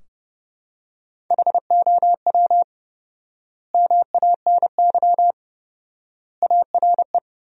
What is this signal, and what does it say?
Morse code, heard as a keyed tone, about 30 wpm, 700 Hz. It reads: HOW MANY ARE
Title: 4x Word-Spacing - Sentences from Top 300 Words - 30wpm